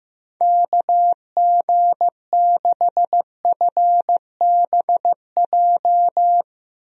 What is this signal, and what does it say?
Morse code, heard as a keyed tone, about 15 wpm, 700 Hz. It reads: KG6FBJ
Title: USA Call Signs - 6 Characters - Repeated 2x - 15wpm